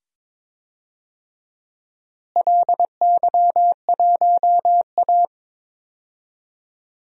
Morse code, heard as a keyed tone, about 22 wpm, 700 Hz. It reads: LY1A
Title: DXCC-100 Call Signs - 4 Characters - Repeated 2x - 22wpm